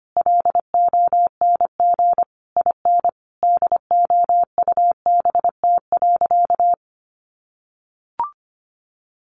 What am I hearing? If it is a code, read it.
Morse code, heard as a keyed tone, about 25 wpm, 700 Hz. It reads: LONG IN DOUBT.
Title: Sentences from Top 1500 Words 25wpm